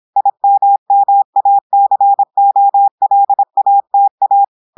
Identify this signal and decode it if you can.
Morse code, heard as a keyed tone, about 26 wpm, 800 Hz. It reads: IMMACOLATA
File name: audios / portugais / portugais texte en clair 1 vitesse 26 mots minutes